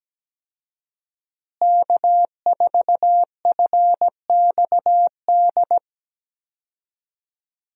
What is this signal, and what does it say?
Morse code, heard as a keyed tone, about 17 wpm, 700 Hz. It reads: K4FXD
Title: USA Call Signs - 5 Characters - Repeated 3x - 17wpm